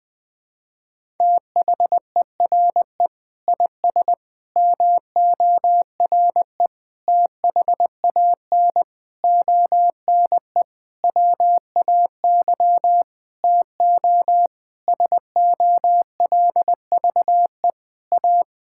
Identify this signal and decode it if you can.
Morse code, heard as a keyed tone, about 20 wpm, 700 Hz. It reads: THERE IS MORE THAN ONE WAY TO SOLVE A
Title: Sentences with 9-Word Phrases - Spoken Beginning - 20wpm